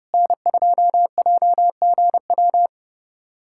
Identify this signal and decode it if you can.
Morse code, heard as a keyed tone, about 30 wpm, 700 Hz. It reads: N2JGW
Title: USA Callsign DE Callsign - 5 Characters - 30wpm